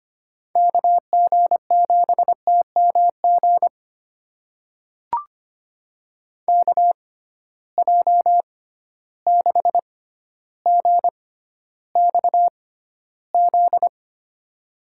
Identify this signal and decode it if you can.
Morse code, heard as a keyed tone, about 25 wpm, spaced as for 13 wpm, 700 Hz. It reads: KG7TMG KJ6GXZ
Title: USA Call Signs - 6 Characters - 6x Character Spacing - 25wpm